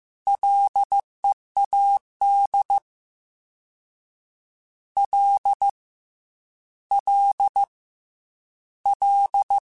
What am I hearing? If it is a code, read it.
Morse code, heard as a keyed tone, about 15 wpm, 800 Hz. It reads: LEAD L L L
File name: Ls / Single L m3s / Single L Prefixe 3x2 Twice @13